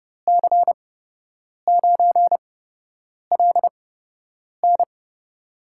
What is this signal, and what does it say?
Morse code, heard as a keyed tone, about 31 wpm, spaced as for 8 wpm, 700 Hz. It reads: C9LN
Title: USA Call Signs - 5 Characters - 8x Character Spacing - 30wpm